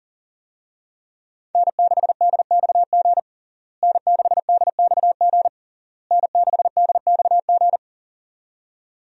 Morse code, heard as a keyed tone, about 37 wpm, 700 Hz. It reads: N6DXG N6DXG N6DXG
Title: USA Call Signs - 5 Characters - Repeated 3x - 40wpm